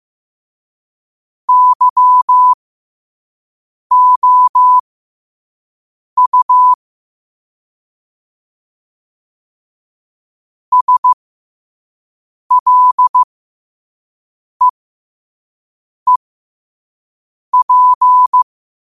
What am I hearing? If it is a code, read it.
Morse code, heard as a keyed tone, about 15 wpm, spaced as for 5 wpm, 1000 Hz. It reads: YOU SLEEP
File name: 5wpm